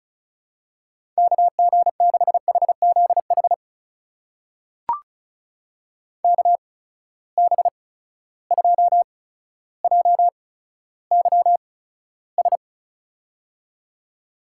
Morse code, heard as a keyed tone, about 36 wpm, spaced as for 15 wpm, 700 Hz. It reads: KG6HZH KB2JYS
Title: USA Call Signs - 6 Characters - 8x Character Spacing - 35wpm